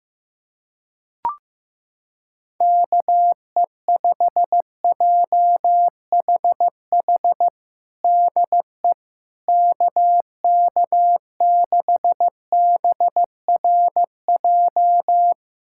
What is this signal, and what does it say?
Morse code, heard as a keyed tone, about 14 wpm, 700 Hz. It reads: KE5JHH DE KK6BRJ
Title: USA Callsign DE Callsign - 6 Characters - 15wpm